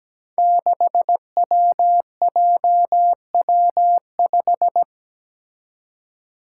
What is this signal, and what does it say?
Morse code, heard as a keyed tone, about 17 wpm, 700 Hz. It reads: 6WJW5
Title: Five Challenging Letter-Number Combinations 17wpm